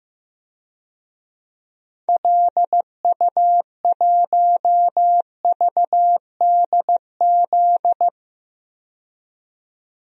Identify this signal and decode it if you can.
Morse code, heard as a keyed tone, about 15 wpm, 700 Hz. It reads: LU1VDZ
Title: DXCC-100 Call Signs - 6 Characters - 15wpm